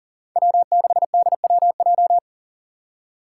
Morse code, heard as a keyed tone, about 40 wpm, 700 Hz. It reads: W6DWJ
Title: Five Challenging Letter-Number Combinations 40wpm